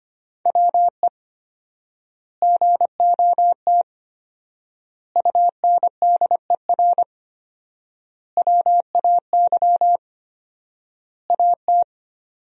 Morse code, heard as a keyed tone, about 25 wpm, 700 Hz. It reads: WE GOT UNDER WAY AT
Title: 4x Word-Spacing - Sentences from Top 300 Words - 25wpm